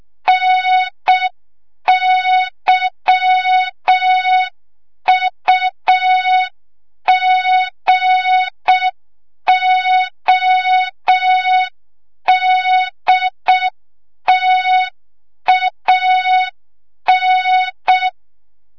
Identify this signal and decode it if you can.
Morse code, heard as a keyed tone, about 6 wpm, 3700 Hz. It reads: NYUGODTAN
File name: Morze-HA2MN / hang / 25 GYAKORLAT